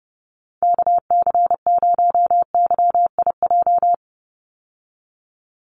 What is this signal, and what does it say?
Morse code, heard as a keyed tone, about 30 wpm, 700 Hz. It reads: KC0YIJ
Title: USA Call Signs 30wpm